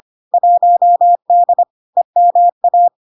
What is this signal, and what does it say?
Morse code, heard as a keyed tone, about 25 wpm, 700 Hz. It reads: 1D EMA
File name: Log-in-CW-25wpm-700-Hz